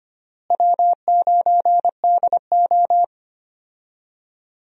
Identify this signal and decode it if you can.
Morse code, heard as a keyed tone, about 25 wpm, 700 Hz. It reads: W9DO
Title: USA Call Signs - 4 Characters - Repeated 3x - 25wpm